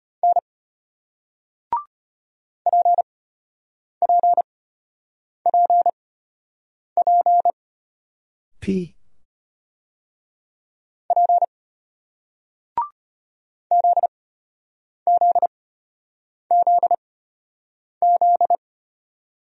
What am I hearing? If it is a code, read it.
Morse code, heard as a keyed tone, about 33 wpm, spaced as for 8 wpm, 700 Hz. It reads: N PPPP P ZZZZ